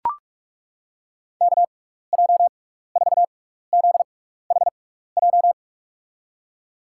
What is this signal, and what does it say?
Morse code, heard as a keyed tone, about 42 wpm, spaced as for 15 wpm, 700 Hz. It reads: KJ4ZHJ
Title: USA Call Signs - 6 Characters - 6x Character Spacing - 45wpm